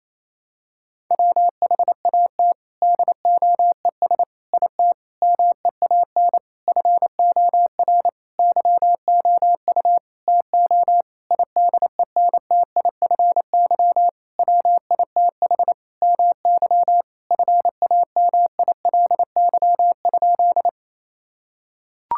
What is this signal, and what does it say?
Morse code, heard as a keyed tone, about 28 wpm, 700 Hz. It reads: WHAT DOES IT MEAN FOR YOU TO IDENTIFY WITH MY FAMILY?
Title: Sentences from Top 1000 Words - Review Part 6 - 28wpm